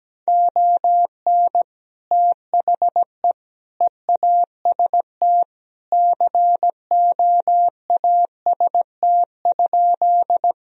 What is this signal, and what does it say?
Morse code, heard as a keyed tone, about 17 wpm, 700 Hz. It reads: ON THE EAST COAST?
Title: Sentences from Top 1000 Words - Review Part 4 - 17wpm